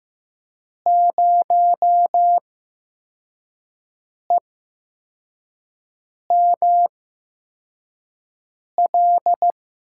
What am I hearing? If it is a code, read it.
Morse code, heard as a keyed tone, about 16 wpm, spaced as for 4 wpm, 700 Hz. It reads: 0EML
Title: USA Call Signs - 6 Characters - 8x Character Spacing - 15wpm